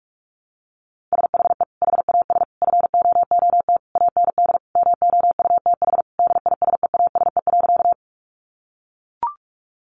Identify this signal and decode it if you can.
Morse code, heard as a keyed tone, about 44 wpm, 700 Hz. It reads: SHE HAS FOOT AND MOUTH DISEASE.
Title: Sentences from Top 1500 Words 45wpm